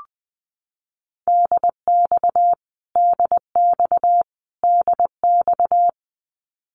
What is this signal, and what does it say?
Morse code, heard as a keyed tone, about 20 wpm, 700 Hz. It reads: DX DX DX
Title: Abbreviations Repeated 3x - 20wpm